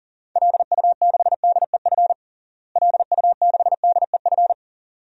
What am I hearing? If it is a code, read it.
Morse code, heard as a keyed tone, about 40 wpm, 700 Hz. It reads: LU6DEF LU6DEF
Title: DXCC-100 Call Signs - 6 Characters - Repeated 2x - 40wpm